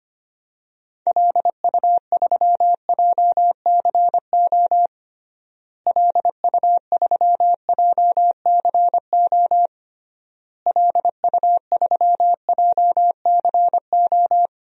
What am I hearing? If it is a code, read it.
Morse code, heard as a keyed tone, about 25 wpm, 700 Hz. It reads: LU3JCO LU3JCO LU3JCO
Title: DXCC-100 Call Signs - 6 Characters - Repeated 3x - 25wpm